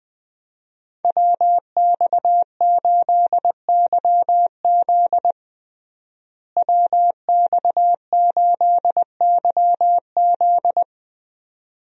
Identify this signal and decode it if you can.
Morse code, heard as a keyed tone, about 20 wpm, 700 Hz. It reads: WX8YZ WX8YZ
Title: USA Call Signs - 5 Characters - Repeated 2x - 20wpm